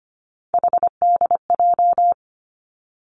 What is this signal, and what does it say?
Morse code, heard as a keyed tone, about 25 wpm, 700 Hz. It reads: HDJ
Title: Three Challenging Letter-Number Combinations 25wpm